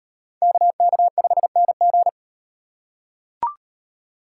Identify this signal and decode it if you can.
Morse code, heard as a keyed tone, about 37 wpm, 700 Hz. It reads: KK5NG